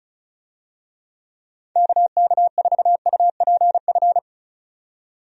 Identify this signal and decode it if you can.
Morse code, heard as a keyed tone, about 36 wpm, 700 Hz. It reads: KK4UPF
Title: USA Call Signs - 6 Characters - 35wpm